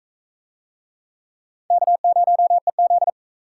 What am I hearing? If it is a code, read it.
Morse code, heard as a keyed tone, about 39 wpm, 700 Hz. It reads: K0EZ